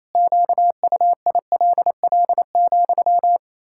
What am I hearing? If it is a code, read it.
Morse code, heard as a keyed tone, about 28 wpm, 700 Hz. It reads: QUILL,
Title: Pangram Challenge v2 - 28wpm